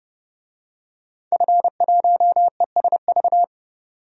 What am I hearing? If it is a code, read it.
Morse code, heard as a keyed tone, about 30 wpm, 700 Hz. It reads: F1ESV
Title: DXCC-100 Call Signs - 5 Characters - 30wpm